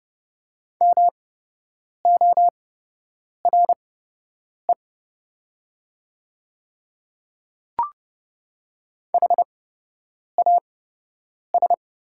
Morse code, heard as a keyed tone, about 29 wpm, spaced as for 8 wpm, 690 Hz. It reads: MORE HAS
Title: 8x Character Spacing - Top 100 Words - 30wpm